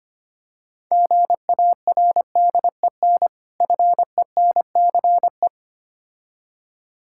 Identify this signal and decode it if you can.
Morse code, heard as a keyed tone, about 26 wpm, 700 Hz. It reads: GARDEN FENCE
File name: practice / 25wpm / Difficult Words - Speed-Racing with Hints - 25wpm